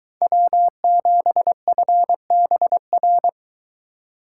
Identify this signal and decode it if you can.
Morse code, heard as a keyed tone, about 22 wpm, 700 Hz. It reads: W7FBR